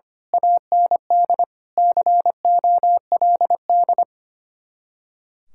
Morse code, heard as a keyed tone, about 24 wpm, 700 Hz. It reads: AND COLD
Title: Speed-Racing Common QSO Phrases - 25wpm